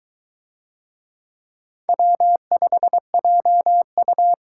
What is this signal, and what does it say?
Morse code, heard as a keyed tone, about 22 wpm, 700 Hz. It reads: W5JU